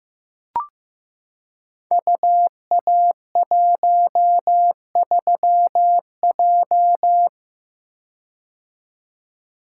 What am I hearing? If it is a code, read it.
Morse code, heard as a keyed tone, about 15 wpm, 700 Hz. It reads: UA13J
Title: Code Groups - 5 Letter:Numbers - 15wpm